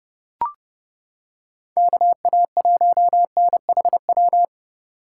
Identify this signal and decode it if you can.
Morse code, heard as a keyed tone, about 28 wpm, 700 Hz. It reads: KA1NHW